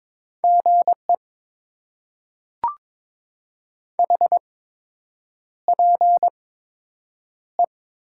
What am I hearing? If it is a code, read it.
Morse code, heard as a keyed tone, about 22 wpm, spaced as for 8 wpm, 700 Hz. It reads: GE HPE
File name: Top 90 QSO Elements - 8x Character Spacing - 22wpm